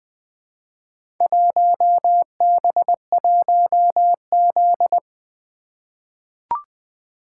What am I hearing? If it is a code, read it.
Morse code, heard as a keyed tone, about 20 wpm, 700 Hz. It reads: 1B1Z
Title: Four Letter-Number Combinations - 20wpm